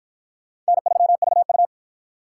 Morse code, heard as a keyed tone, about 53 wpm, 700 Hz. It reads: N3VU